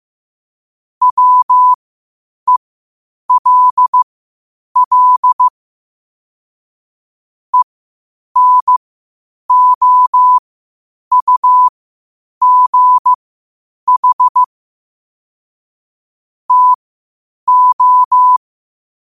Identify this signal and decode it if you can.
Morse code, heard as a keyed tone, about 15 wpm, spaced as for 8 wpm, 1000 Hz. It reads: WELL ENOUGH TO
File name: qotdfiles / 8wpm